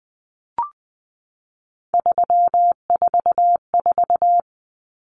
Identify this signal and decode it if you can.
Morse code, heard as a keyed tone, about 20 wpm, 700 Hz. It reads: 344
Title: Numbers - 3-Digit 20wpm